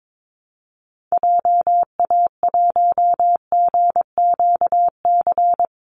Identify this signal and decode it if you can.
Morse code, heard as a keyed tone, about 22 wpm, 700 Hz. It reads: JA1GQC
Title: Call Signs - 22wpm